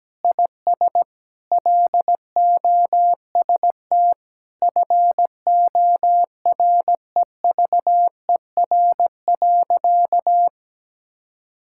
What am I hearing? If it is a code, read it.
Morse code, heard as a keyed tone, about 17 wpm, 700 Hz. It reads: IS LOST FOREVER.
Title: Sentences from Top 1600 Words 17wpm